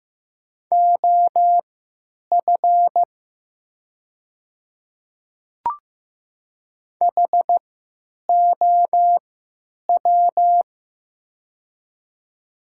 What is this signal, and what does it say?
Morse code, heard as a keyed tone, about 15 wpm, spaced as for 9 wpm, 700 Hz. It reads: OF HOW
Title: 3x Character Spacing - Top 100 Words - 15wpm